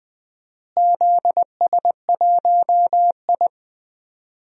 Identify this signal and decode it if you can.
Morse code, heard as a keyed tone, about 20 wpm, 700 Hz. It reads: ZS1I
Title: DXCC-100 Call Signs - 4 Characters - Repeated 2x - 20wpm